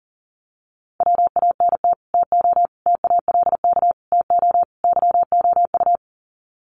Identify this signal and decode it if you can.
Morse code, heard as a keyed tone, about 40 wpm, 700 Hz. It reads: WANT TO TALK TO YOU
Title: Sets of 5 Words Encore 40wpm